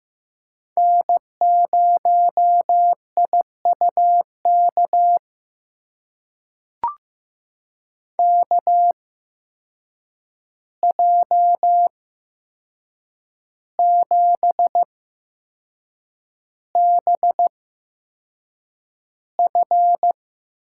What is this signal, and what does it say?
Morse code, heard as a keyed tone, about 15 wpm, spaced as for 6 wpm, 700 Hz. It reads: N0IUK KJ7BF
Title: USA Call Signs - 8x Character Spacing - 15wpm